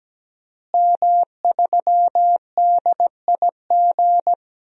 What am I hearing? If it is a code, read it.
Morse code, heard as a keyed tone, about 17 wpm, 700 Hz. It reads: M3DIG
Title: DXCC-100 Call Signs - 5 Characters - Repeated 3x - 17wpm